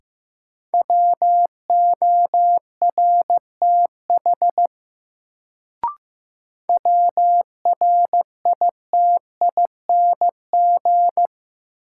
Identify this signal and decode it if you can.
Morse code, heard as a keyed tone, about 15 wpm, 700 Hz. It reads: WORTH WRITING
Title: Sentences from Top 800 Words - New Word Drill - 15wpm